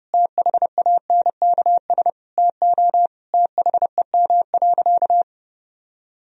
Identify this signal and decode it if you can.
Morse code, heard as a keyed tone, about 30 wpm, 700 Hz. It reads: THANKS TO THEM.
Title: Sentences from Top 1000 Words - Review Part 3 - 30wpm